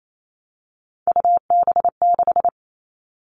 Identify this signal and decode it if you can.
Morse code, heard as a keyed tone, about 28 wpm, 700 Hz. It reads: UB6
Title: All Cut Numbers - 3-Digit 28wpm